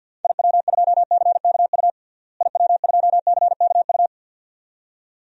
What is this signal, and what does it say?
Morse code, heard as a keyed tone, about 52 wpm, 700 Hz. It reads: IW2CKU IW2CKU
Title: DXCC-100 Call Signs - 6 Characters - Repeated 2x - 50wpm